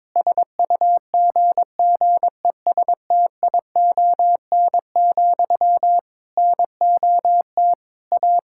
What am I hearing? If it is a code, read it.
Morse code, heard as a keyed tone, about 22 wpm, 700 Hz. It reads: SUGGESTION, NOT A
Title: Sentences from Top 1600 Words 22wpm